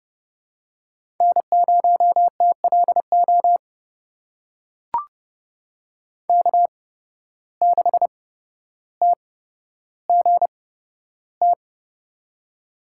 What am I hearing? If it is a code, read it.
Morse code, heard as a keyed tone, about 30 wpm, spaced as for 13 wpm, 700 Hz. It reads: N0TLO K6TGT
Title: USA Call Signs - 5 Characters - 8x Character Spacing - 30wpm